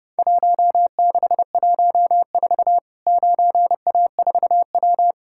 Morse code, heard as a keyed tone, about 30 wpm, 700 Hz. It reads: 1614 9A4W
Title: Worked All Europe DX Contest - QTC Exchange - 30wpm